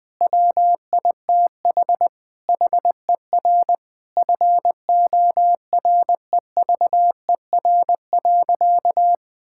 Morse code, heard as a keyed tone, about 20 wpm, 700 Hz. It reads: WITH HER FOREVER.
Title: Sentences from Top 1600 Words 20wpm